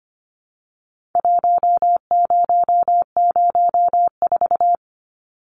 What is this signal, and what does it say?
Morse code, heard as a keyed tone, about 25 wpm, 700 Hz. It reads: 1004
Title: Numbers - 4-Digit 25wpm